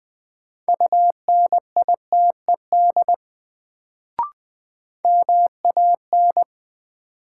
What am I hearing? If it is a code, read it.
Morse code, heard as a keyed tone, about 20 wpm, 700 Hz. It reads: UNITED MAN
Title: Sets of 2 Words - New Word Drill - 20wpm